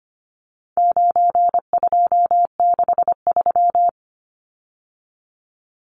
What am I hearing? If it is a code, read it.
Morse code, heard as a keyed tone, about 25 wpm, 700 Hz. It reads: 9263
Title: Numbers - 4-Digit 25wpm